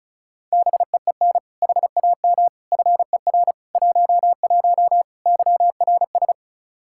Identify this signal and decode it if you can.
Morse code, heard as a keyed tone, about 36 wpm, 700 Hz. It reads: BEEN HAM FER 11 YRS
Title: Practice QSO - Exchange 3 - More Info - 35wpm